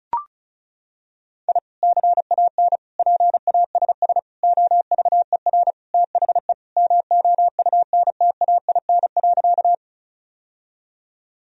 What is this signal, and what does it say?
Morse code, heard as a keyed tone, about 36 wpm, 700 Hz. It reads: I CAN PASS OVER THE MOUNTAIN.
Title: Sentences from Top 1000 Words - Review Part 1 - 35wpm